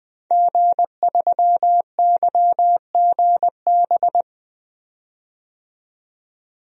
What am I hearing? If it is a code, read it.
Morse code, heard as a keyed tone, about 20 wpm, 700 Hz. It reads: G3YGB
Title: DXCC-100 Call Signs - 5 Characters - 20wpm